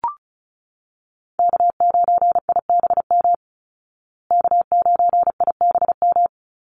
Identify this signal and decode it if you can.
Morse code, heard as a keyed tone, about 34 wpm, 700 Hz. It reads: K9IBM K9IBM
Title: Call Signs Repeated 2x - 35wpm